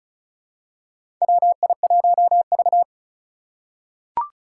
Morse code, heard as a keyed tone, about 36 wpm, 700 Hz. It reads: WI1V
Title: USA Call Signs - 2x Character Spacing - 35wpm